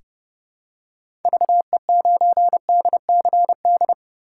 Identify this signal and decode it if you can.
Morse code, heard as a keyed tone, about 29 wpm, 700 Hz. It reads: VE9DCD
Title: K1USN SST QSO Practice - 30wpm